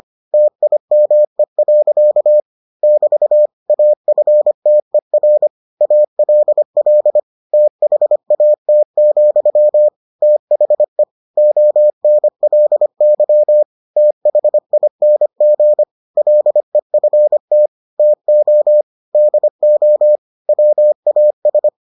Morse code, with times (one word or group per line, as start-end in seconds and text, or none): 0.3-2.4 TIME.
2.8-3.5 <BT>
3.7-5.5 AFTER
5.8-7.2 ALL
7.5-9.9 THAT,
10.2-11.0 THE
11.4-13.6 ONLY
14.0-15.8 THING
16.2-17.7 LEFT
18.0-18.8 TO
19.1-20.2 DO
20.5-21.7 WAS